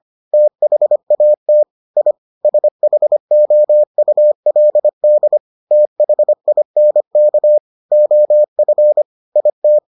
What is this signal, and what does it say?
Morse code, heard as a keyed tone, about 25 wpm, 600 Hz. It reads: THAT I SHOULD THINK OF IT